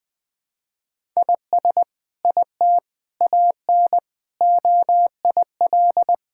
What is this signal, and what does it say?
Morse code, heard as a keyed tone, about 21 wpm, 700 Hz. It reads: IS IT AN OIL
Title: Sentences from Top 1000 Words - Review Part 6 - 20wpm